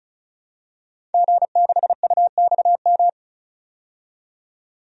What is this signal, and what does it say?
Morse code, heard as a keyed tone, about 36 wpm, 700 Hz. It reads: G6UXM
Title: DXCC-100 Call Signs - 5 Characters - 35wpm